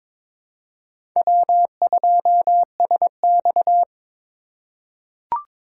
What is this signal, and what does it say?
Morse code, heard as a keyed tone, about 22 wpm, 700 Hz. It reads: W2SX
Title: USA Call Signs - 4 Characters - 4x Character Spacing - 22wpm